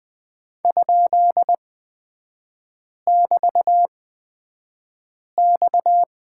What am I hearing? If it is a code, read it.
Morse code, heard as a keyed tone, about 20 wpm, spaced as for 5 wpm, 690 Hz. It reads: ?<BT>X
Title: 229 Lesson 37e - Letter x - Three Characters 20wpm